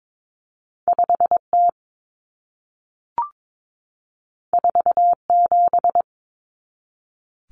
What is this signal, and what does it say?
Morse code, heard as a keyed tone, about 22 wpm, 700 Hz. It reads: 5T 47
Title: Common Cut Numbers - 2-Digit 22wpm